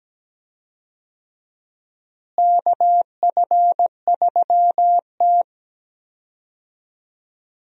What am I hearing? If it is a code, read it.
Morse code, heard as a keyed tone, about 17 wpm, 700 Hz. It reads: KF3T
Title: USA Call Signs - 4 Characters - Repeated 3x - 17wpm